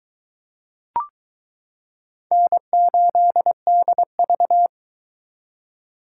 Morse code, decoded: N8DV

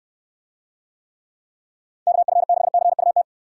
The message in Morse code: KF6CUT

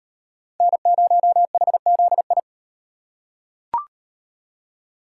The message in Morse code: N0HZI

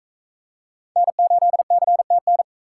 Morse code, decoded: N8CTN

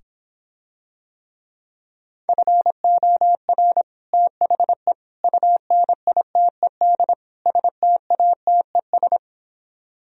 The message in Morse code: FOR THE UNITED STATES

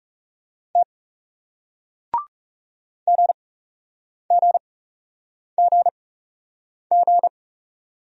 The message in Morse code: T GGGG

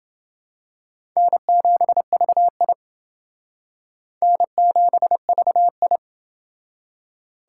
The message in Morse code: N7VI N7VI

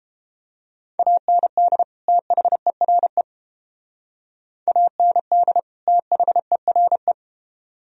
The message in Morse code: AND THERE AND THERE

